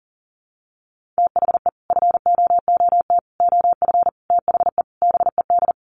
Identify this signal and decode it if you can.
Morse code, heard as a keyed tone, about 40 wpm, 700 Hz. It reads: THE FOOT OF THE BED
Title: Sets of 5 Words Encore2 40wpm